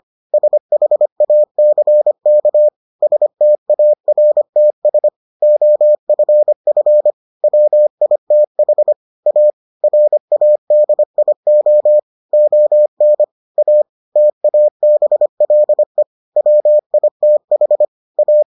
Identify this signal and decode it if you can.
Morse code, heard as a keyed tone, about 25 wpm, 600 Hz. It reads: SHACK STARTS OFF WITH A RADIO ON A TABLE WITH A